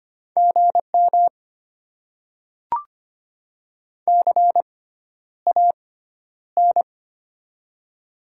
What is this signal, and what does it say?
Morse code, heard as a keyed tone, about 25 wpm, spaced as for 11 wpm, 700 Hz. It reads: GM CAN
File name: Top 90 QSO Elements - 6x Character Spacing - 25wpm